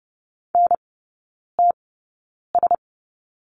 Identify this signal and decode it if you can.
Morse code, heard as a keyed tone, about 30 wpm, spaced as for 9 wpm, 700 Hz. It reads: NTS
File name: Lesson 16e - Letter c - Three Characters 30wpm